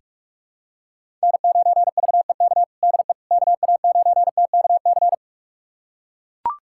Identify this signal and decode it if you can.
Morse code, heard as a keyed tone, about 41 wpm, 700 Hz. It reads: N9VEK DE KA9TKC
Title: USA Callsign DE Callsign 45wpm